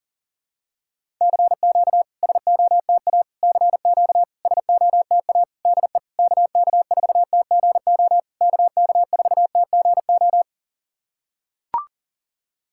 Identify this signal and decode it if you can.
Morse code, heard as a keyed tone, about 40 wpm, 700 Hz. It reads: CQ SOTA CQ SOTA DE KK4TGO KK4TGO
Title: SOTA QSO Practice 40wpm